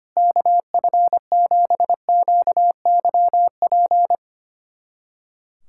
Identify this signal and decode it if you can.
Morse code, heard as a keyed tone, about 25 wpm, 700 Hz. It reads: KF7QYP
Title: USA Callsign DE Callsign - 6 Characters - 25wpm